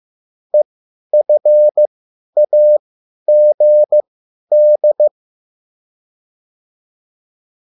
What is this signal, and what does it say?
Morse code, heard as a keyed tone, about 15 wpm, spaced as for 10 wpm, 600 Hz. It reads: EFAGD